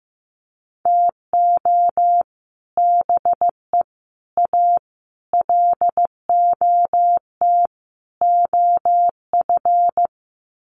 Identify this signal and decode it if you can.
Morse code, heard as a keyed tone, about 15 wpm, 700 Hz. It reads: TO BE A LOT OF
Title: Sets of 5 Words Encore 15wpm